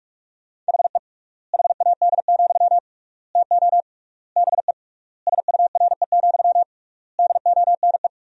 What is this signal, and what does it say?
Morse code, heard as a keyed tone, about 46 wpm, 700 Hz. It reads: HE HAD, TO BE SURE, DONE
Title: 3x Word-Spacing - Sentences from Top 500 Words - 45wpm